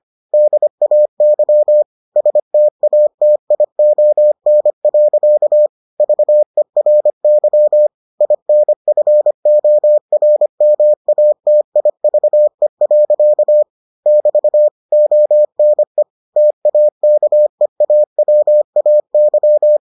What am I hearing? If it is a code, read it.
Morse code, heard as a keyed tone, about 25 wpm, 600 Hz. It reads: DAY STATION. VERY INFORMATIVE. <BT> ONE TAKEAWAY